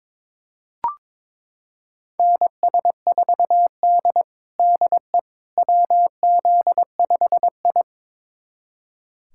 Morse code, heard as a keyed tone, about 22 wpm, 700 Hz. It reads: NS4D DE WZ5I
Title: USA Callsign DE Callsign - 4 Characters - 22wpm